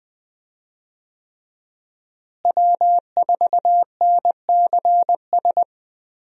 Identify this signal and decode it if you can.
Morse code, heard as a keyed tone, about 20 wpm, 700 Hz. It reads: W4NCS
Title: USA Call Signs - 5 Characters - Repeated 3x - 20wpm